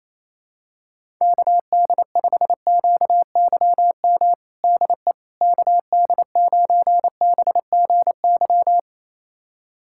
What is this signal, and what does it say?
Morse code, heard as a keyed tone, about 28 wpm, 700 Hz. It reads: KD5QYM DE KD9BGY
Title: USA Callsign DE Callsign - 6 Characters - 28wpm